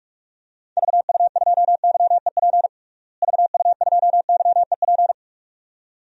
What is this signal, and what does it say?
Morse code, heard as a keyed tone, about 42 wpm, 700 Hz. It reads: VU2YEP VU2YEP
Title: DXCC-100 Call Signs - 6 Characters - Repeated 2x - 45wpm